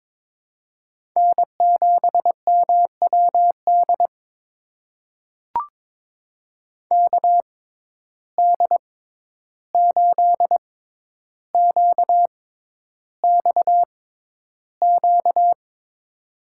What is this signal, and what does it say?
Morse code, heard as a keyed tone, about 22 wpm, spaced as for 11 wpm, 700 Hz. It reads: N7MWD KD8QXQ
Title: USA Call Signs - 6x Character Spacing - 22wpm